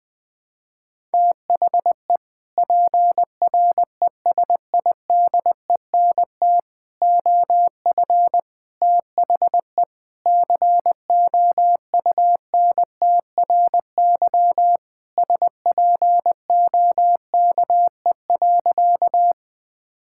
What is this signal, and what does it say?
Morse code, heard as a keyed tone, about 21 wpm, 700 Hz. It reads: THE PRESIDENT OF THE COUNTRY SPOKE.
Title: Sentences with 3-Word Phrases - Spoken Beginning - 20wpm